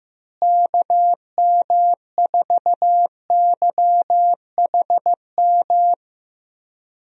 Morse code, heard as a keyed tone, about 15 wpm, 700 Hz. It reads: KM4YHM
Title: USA Call Signs - 6 Characters - Repeated 3x - 15wpm